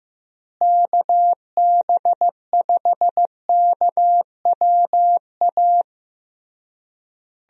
Morse code, heard as a keyed tone, about 15 wpm, 700 Hz. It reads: KB5KWA
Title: USA Call Signs - 6 Characters - Repeated 3x - 15wpm